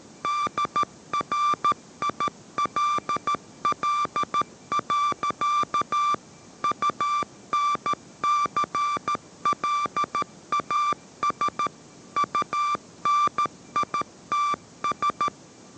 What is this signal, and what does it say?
Morse code, heard as a keyed tone, about 13 wpm, 1170 Hz. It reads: DRILL. UNCLAS UNITS
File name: 14wpm